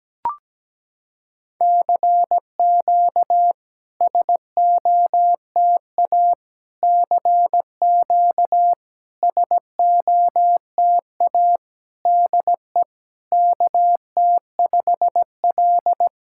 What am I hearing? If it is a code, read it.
Morse code, heard as a keyed tone, about 17 wpm, 700 Hz. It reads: CQ SOTA CQ SOTA DE KT5L
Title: SOTA QSO Practice 17wpm